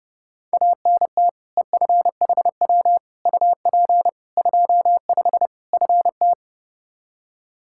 Morse code, heard as a keyed tone, about 30 wpm, 700 Hz. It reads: ANT EFHW UP 25 FT
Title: Practice QSO - Exchange 2 - Info - 30wpm